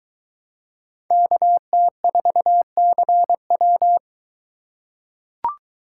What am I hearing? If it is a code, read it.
Morse code, heard as a keyed tone, about 23 wpm, 700 Hz. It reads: KT4CW